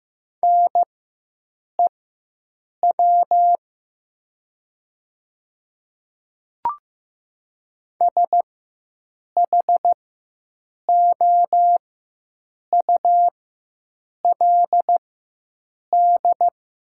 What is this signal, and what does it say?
Morse code, heard as a keyed tone, about 15 wpm, spaced as for 7 wpm, 700 Hz. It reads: NEW SHOULD
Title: 4x Character Spacing - Top 300 Words - 15wpm